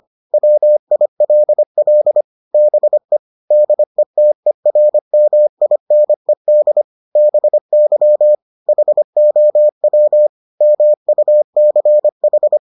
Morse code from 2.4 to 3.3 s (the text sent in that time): BE